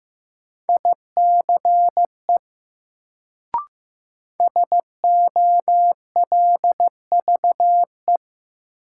Morse code, heard as a keyed tone, about 15 wpm, 700 Hz. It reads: ICE SOLVE
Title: Sentences from Top 1000 Words - New Word Drill - 15wpm